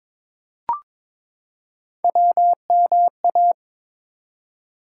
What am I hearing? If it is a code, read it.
Morse code, heard as a keyed tone, about 22 wpm, 700 Hz. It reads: WMA
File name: ARRL Sections v2 - 22wpm